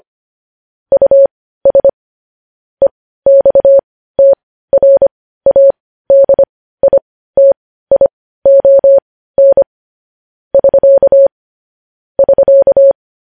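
Morse code, heard as a keyed tone, about 25 wpm, spaced as for 15 wpm, 550 Hz. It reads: US EXTRADITION <SK> <SK>
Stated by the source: Morse news headlines practice at 15 wpm.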